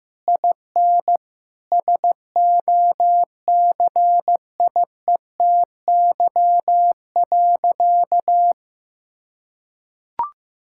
Morse code, heard as a keyed tone, about 15 wpm, 700 Hz. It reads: IN SOCIETY.
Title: Sentences from Top 1600 Words 15wpm